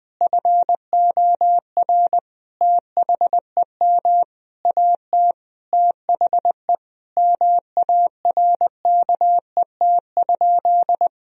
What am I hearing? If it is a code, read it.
Morse code, heard as a keyed tone, about 19 wpm, 700 Hz. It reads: FOR THEM AT THE MARKET?
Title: Sentences from Top 1000 Words - Review Part 3 - 20wpm